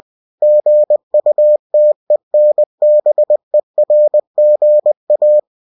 Morse code, heard as a keyed tone, about 20 wpm, 600 Hz. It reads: GUTENBERGA
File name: cw-books / winnie-the-pooh / winnie-20wpm-page-0114